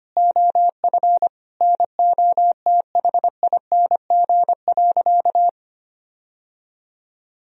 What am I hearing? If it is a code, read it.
Morse code, heard as a keyed tone, about 25 wpm, 700 Hz. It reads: OF NOTHING.
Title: Sentences from Top 1000 Words - Review Part 3 - 25wpm